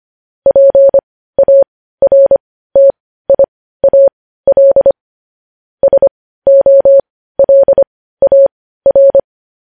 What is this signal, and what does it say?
Morse code, heard as a keyed tone, about 25 wpm, spaced as for 15 wpm, 550 Hz. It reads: PARTIAL SOLAR
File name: podcast / News headlines-1x-25x15